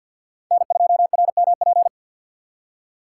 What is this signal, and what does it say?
Morse code, heard as a keyed tone, about 47 wpm, 700 Hz. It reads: N2RMP
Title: USA Call Signs - 5 Characters - 2x Character Spacing - 50wpm